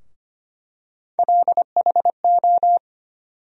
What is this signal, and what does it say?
Morse code, heard as a keyed tone, about 25 wpm, 700 Hz. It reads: LHO
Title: USA Call Sign Postfixes - 3 Letters 25wpm